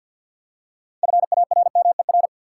Morse code, heard as a keyed tone, about 51 wpm, 700 Hz. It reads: FARMER
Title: Six Letter Words - 50wpm